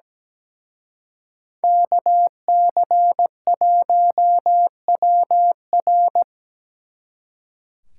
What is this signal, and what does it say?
Morse code, heard as a keyed tone, about 20 wpm, 700 Hz. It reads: KC1WR
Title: Speed Racing - USA 2x2 Call Signs 17wpm